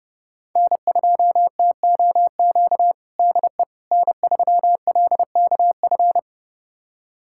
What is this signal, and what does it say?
Morse code, heard as a keyed tone, about 30 wpm, 700 Hz. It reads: N2TOQ DE N3LKF
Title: USA Callsign DE Callsign - 5 Characters - 30wpm